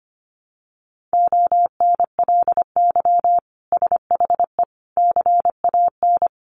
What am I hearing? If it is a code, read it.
Morse code, heard as a keyed tone, about 25 wpm, 700 Hz. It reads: ONLY SHE CAN
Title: Sentences from Top 1100 Words 25wpm